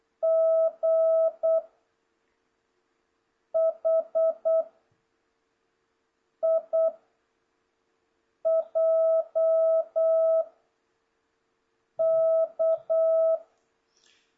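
Morse code, heard as a keyed tone, about 8 wpm, 660 Hz. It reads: G H I J K
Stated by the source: Morse code lesson from JH8CBH's Zoom Morse training course (ZOOMモールス講習会), session 2 teaching letters G to K.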